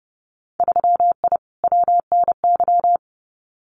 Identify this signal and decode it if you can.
Morse code, heard as a keyed tone, about 30 wpm, 700 Hz. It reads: 3I WNY
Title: Winter Field Day Exchanges 30wpm